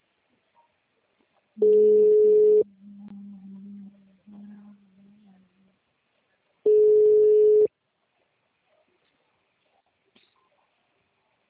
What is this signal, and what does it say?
Ringback tone (Europe, 425 Hz).